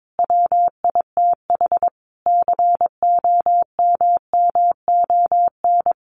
Morse code, heard as a keyed tone, about 22 wpm, 700 Hz. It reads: WITH COMMON
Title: Sentences from Top 1300 Words 22wpm